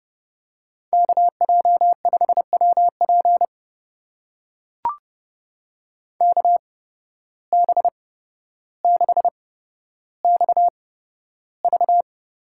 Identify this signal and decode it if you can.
Morse code, heard as a keyed tone, about 30 wpm, spaced as for 13 wpm, 700 Hz. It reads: KJ5WP KB6XV
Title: USA Call Signs - 5 Characters - 8x Character Spacing - 30wpm